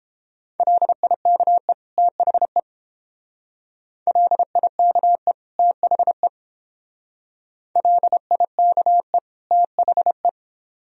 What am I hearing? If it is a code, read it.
Morse code, heard as a keyed tone, about 30 wpm, 700 Hz. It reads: LIKE THE LIKE THE LIKE THE